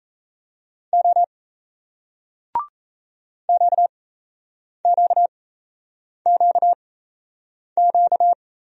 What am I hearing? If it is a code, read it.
Morse code, heard as a keyed tone, about 38 wpm, spaced as for 8 wpm, 700 Hz. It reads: O QQQQ